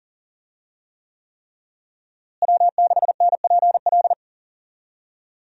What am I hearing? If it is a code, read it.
Morse code, heard as a keyed tone, about 40 wpm, 700 Hz. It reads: W6NPL
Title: USA Call Signs - 5 Characters - Repeated 3x - 40wpm